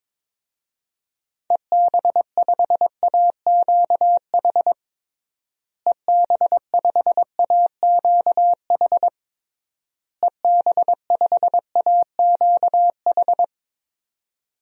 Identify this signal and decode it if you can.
Morse code, heard as a keyed tone, about 22 wpm, 700 Hz. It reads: EB5AQH EB5AQH EB5AQH
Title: DXCC-100 Call Signs - 6 Characters - Repeated 3x - 22wpm